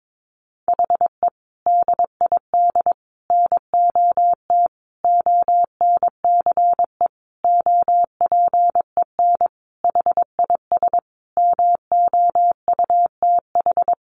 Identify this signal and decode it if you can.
Morse code, heard as a keyed tone, about 22 wpm, 700 Hz. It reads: HE DID NOT ONCE OPEN HIS MOUTH
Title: Sentences from Top 1500 Words 22wpm